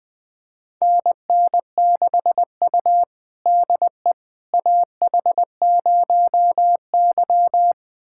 Morse code, heard as a keyed tone, about 20 wpm, 700 Hz. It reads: NN6U DE AH0Y
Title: USA Callsign DE Callsign - 4 Characters - 20wpm